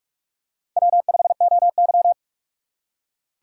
Morse code, heard as a keyed tone, about 42 wpm, 700 Hz. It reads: W5OY